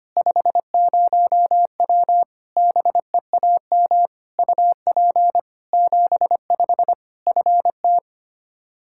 Morse code, heard as a keyed tone, about 25 wpm, 700 Hz. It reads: 50W BEAM UP 75 FT
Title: Practice QSO - Exchange 2 - Info - 25wpm